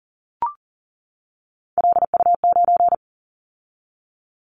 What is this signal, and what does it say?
Morse code, heard as a keyed tone, about 41 wpm, 700 Hz. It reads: LU9
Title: Callsign Prefixes for Top 50 DXCC - 40wpm